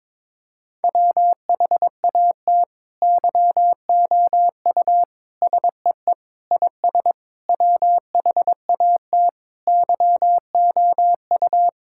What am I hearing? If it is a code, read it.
Morse code, heard as a keyed tone, about 22 wpm, 700 Hz. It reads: WHAT YOU SEE IS WHAT YOU
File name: Sentences with 7-Word Phrases - 22wpm